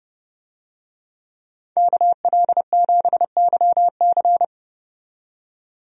Morse code, heard as a keyed tone, about 30 wpm, 700 Hz. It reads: KL7YC
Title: DXCC-100 Call Signs - 5 Characters - 30wpm